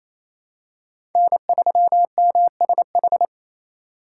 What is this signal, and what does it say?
Morse code, heard as a keyed tone, about 28 wpm, 700 Hz. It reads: N3MSH
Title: USA Call Signs - 5 Characters - 2x Character Spacing - 28wpm